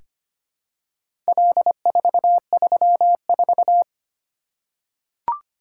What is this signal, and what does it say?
Morse code, heard as a keyed tone, about 25 wpm, 700 Hz. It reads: L434
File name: practice / 25wpm / Four Letter-Number Combinations - 25wpm